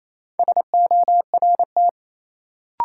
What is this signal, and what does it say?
Morse code, heard as a keyed tone, about 27 wpm, 700 Hz. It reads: SORT
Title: Sentences from Top 1300 Words - New Word Drill Alt - 28wpm